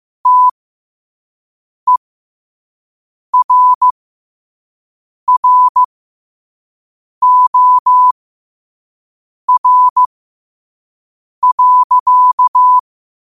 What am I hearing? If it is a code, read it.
Morse code, heard as a keyed tone, about 15 wpm, spaced as for 5 wpm, 1000 Hz. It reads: TERROR.
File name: qotdfiles / 5wpm